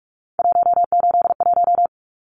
Morse code, heard as a keyed tone, about 44 wpm, 700 Hz. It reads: 181
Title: Numbers - 3-Digit 45wpm